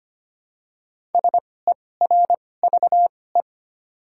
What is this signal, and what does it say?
Morse code, heard as a keyed tone, about 25 wpm, spaced as for 18 wpm, 700 Hz. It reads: SERVE
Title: 2x Character Spacing - Top 500 Words - 25wpm